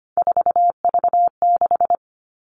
Morse code, heard as a keyed tone, about 25 wpm, 700 Hz. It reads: 4V6
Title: All Cut Numbers - 3-Digit 25wpm